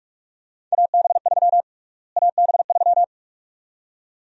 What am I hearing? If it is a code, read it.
Morse code, heard as a keyed tone, about 47 wpm, 700 Hz. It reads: AB3 AB3
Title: USA Call Sign Prefixes - Repeated 2x - 45wpm